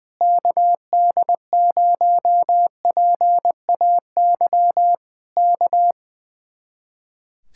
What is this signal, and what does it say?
Morse code, heard as a keyed tone, about 20 wpm, 700 Hz. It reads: KD0PAY K
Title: Speed-Racing Common QSO Phrases Encore - 20wpm